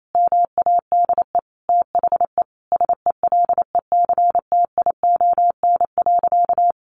Morse code, heard as a keyed tone, about 28 wpm, 700 Hz. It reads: MADE THE SELECTION.
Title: Sentences from Top 1400 Words 28wpm